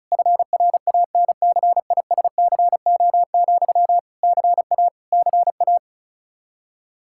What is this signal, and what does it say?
Morse code, heard as a keyed tone, about 36 wpm, 700 Hz. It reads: FRANCISCO, CA CA
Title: Practice QSO - Exchange 1 - Beginning - 35wpm